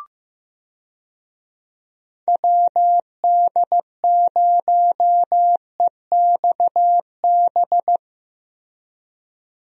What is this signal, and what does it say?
Morse code, heard as a keyed tone, about 15 wpm, 700 Hz. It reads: WD0EXB
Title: USA Call Signs - 6 Characters - Repeated 2x - 15wpm